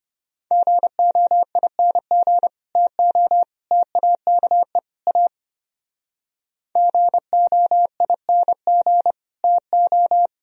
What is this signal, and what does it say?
Morse code, heard as a keyed tone, about 30 wpm, 700 Hz. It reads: GOING TO TAKE A GOING TO
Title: Speed-Racing - Sets of 4 Words 25wpm